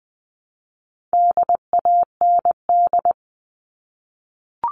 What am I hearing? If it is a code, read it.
Morse code, heard as a keyed tone, about 20 wpm, 700 Hz. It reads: DAND
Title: All Cut Numbers - 4-Digit 20wpm